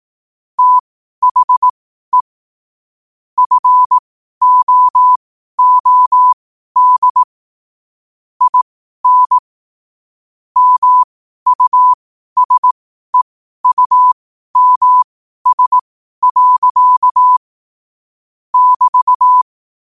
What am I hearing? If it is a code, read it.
Morse code, heard as a keyed tone, about 18 wpm, spaced as for 12 wpm, 1000 Hz. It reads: THE FOOD IN MUSEUMS. <BT>
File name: 12wpm